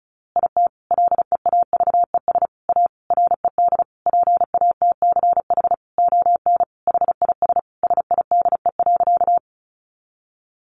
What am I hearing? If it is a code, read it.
Morse code, heard as a keyed tone, about 34 wpm, 700 Hz. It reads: IT LEAVES A RED PATCH ON HIS SIDE.
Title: A Sound Of Thunder 35wpm